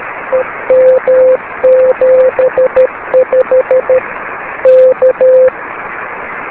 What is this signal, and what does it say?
Morse code, heard as a keyed tone, about 13 wpm, 520 Hz, with radio noise. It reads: EM75 K
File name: beacons / k4kwk-b